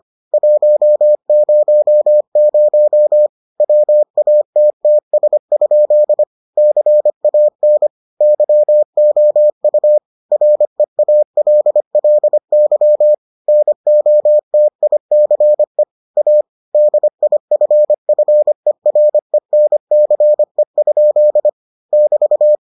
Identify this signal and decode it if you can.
Morse code, heard as a keyed tone, about 25 wpm, 600 Hz. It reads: 100 WATTS? CAN YOU REALLY NOTICE A DIFFERENCE? <BT>